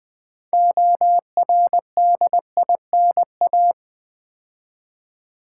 Morse code, heard as a keyed tone, about 20 wpm, 700 Hz. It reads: ORDINA
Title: Build A Word - 2 Letters At A Time - 8-Letter Words - 20wpm